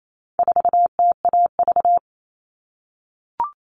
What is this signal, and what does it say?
Morse code, heard as a keyed tone, about 28 wpm, 700 Hz. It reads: 4TAV
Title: All Cut Numbers - 4-Digit 28wpm